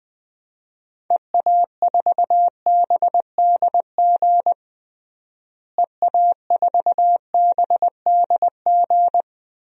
Morse code, heard as a keyed tone, about 20 wpm, 700 Hz. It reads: EA4BDG EA4BDG
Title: DXCC-100 Call Signs - 6 Characters - Repeated 3x - 20wpm